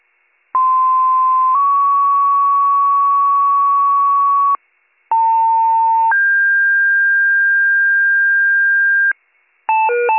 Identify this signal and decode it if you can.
Two-tone page, heard as a one-second tone, then a three-second tone, about 1040 then 1120 Hz, about 900 then 1600 Hz. It is heard from 0.5 to 9.1 s.